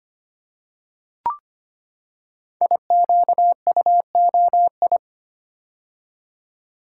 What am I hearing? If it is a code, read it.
Morse code, heard as a keyed tone, about 25 wpm, 700 Hz. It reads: IQUOI
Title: Code Groups - 5 Letters - 25wpm